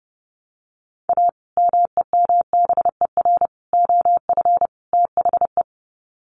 Morse code, heard as a keyed tone, about 30 wpm, 700 Hz. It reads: A MEMBER OF THE
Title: Sets of 4 Words 30wpm